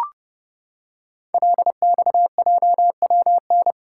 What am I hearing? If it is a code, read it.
Morse code, heard as a keyed tone, about 28 wpm, 700 Hz. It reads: LXJWN
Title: Code Groups - 5 Letter:Numbers - 30wpm